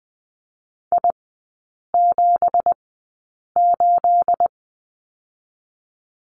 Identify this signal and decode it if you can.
Morse code, heard as a keyed tone, about 20 wpm, spaced as for 8 wpm, 690 Hz. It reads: I78
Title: 212 Lesson 34e - Number 8 - Three Characters 20wpm